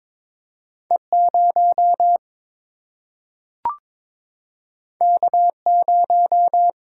Morse code, heard as a keyed tone, about 22 wpm, 700 Hz. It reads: E 0 K 0
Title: Single-Letter Single-Number 22wpm